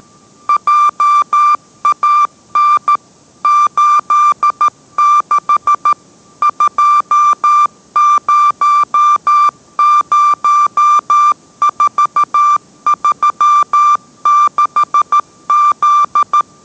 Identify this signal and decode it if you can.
Morse code, heard as a keyed tone, about 13 wpm, 1180 Hz. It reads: JAN 86 200436Z